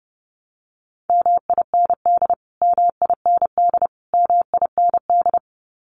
Morse code, heard as a keyed tone, about 29 wpm, 700 Hz. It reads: MIND MIND MIND
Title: Top 500 Words Repeated 30wpm